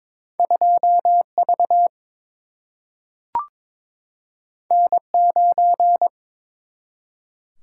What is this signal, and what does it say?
Morse code, heard as a keyed tone, about 21 wpm, 700 Hz. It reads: 2V N9
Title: Single-Letter Single-Number 22wpm